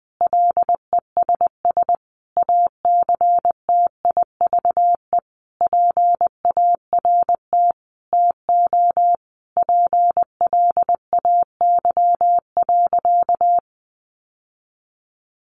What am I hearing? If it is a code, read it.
Morse code, heard as a keyed tone, about 20 wpm, 700 Hz. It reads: LESS ACTIVE PART TO PLAY.
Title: Sentences from Top 900 Words 20wpm